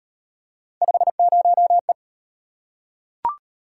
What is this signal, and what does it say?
Morse code, heard as a keyed tone, about 38 wpm, 700 Hz. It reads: 50E